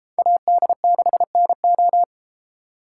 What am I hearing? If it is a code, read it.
Morse code, heard as a keyed tone, about 32 wpm, 700 Hz. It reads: AD6NO